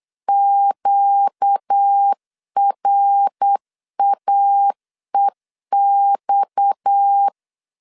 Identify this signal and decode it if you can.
Morse code, heard as a keyed tone, about 8 wpm, 800 Hz. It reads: QRAEX
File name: LEZIONI / 10 SIGNS / 10C DOT 8wpm 3649